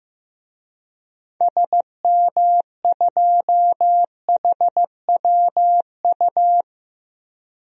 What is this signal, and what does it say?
Morse code, heard as a keyed tone, about 15 wpm, 690 Hz. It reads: SM2HWU
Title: DXCC-100 Call Signs - 6 Characters - 15wpm